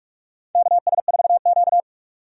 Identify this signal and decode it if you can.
Morse code, heard as a keyed tone, about 46 wpm, 700 Hz. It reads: KS4Q
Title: USA Call Signs - 4 Characters - 45wpm